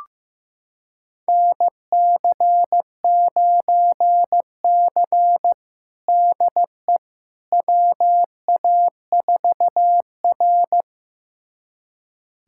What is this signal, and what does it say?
Morse code, heard as a keyed tone, about 15 wpm, 700 Hz. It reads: NC9C DE WA4R
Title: USA Callsign DE Callsign - 4 Characters - 15wpm